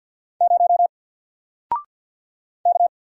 Morse code, heard as a keyed tone, about 48 wpm, 700 Hz. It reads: 0 K
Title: Single Letter-Number Alt 50wpm